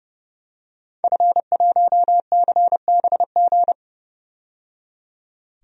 Morse code, heard as a keyed tone, about 32 wpm, 700 Hz. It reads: F1CBG
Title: DXCC-100 Call Signs - 5 Characters - 30wpm